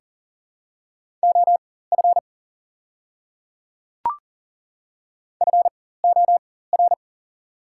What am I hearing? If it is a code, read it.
Morse code, heard as a keyed tone, about 42 wpm, spaced as for 19 wpm, 690 Hz. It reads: OF FOR
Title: 4x Character Spacing - Top 100 Words - 40wpm